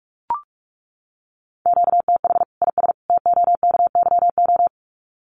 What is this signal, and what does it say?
Morse code, heard as a keyed tone, about 44 wpm, 700 Hz. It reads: QTH IS TOKYO
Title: Common QSO Phrases 45wpm